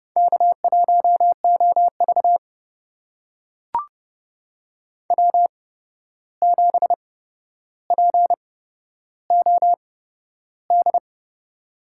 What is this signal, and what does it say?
Morse code, heard as a keyed tone, about 28 wpm, spaced as for 12 wpm, 700 Hz. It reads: K1OV W7POD
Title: USA Call Signs - 8x Character Spacing - 30wpm